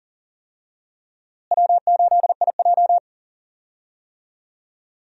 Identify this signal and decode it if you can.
Morse code, heard as a keyed tone, about 40 wpm, 700 Hz. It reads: W8IJ
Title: USA Call Signs - 4 Characters - Repeated 3x - 40wpm